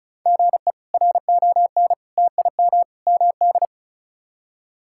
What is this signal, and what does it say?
Morse code, heard as a keyed tone, about 34 wpm, 700 Hz. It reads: GE RON TIM MD
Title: K1USN SST QSO Practice - 35wpm